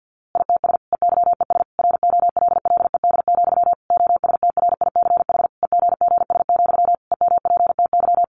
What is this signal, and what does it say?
Morse code, heard as a keyed tone, about 50 wpm, 700 Hz. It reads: ITS EYES ROLLED, OSTRICH EGGS, EMPTY
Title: A Sound Of Thunder 50wpm